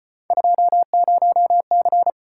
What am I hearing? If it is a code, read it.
Morse code, heard as a keyed tone, about 34 wpm, 700 Hz. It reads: 20C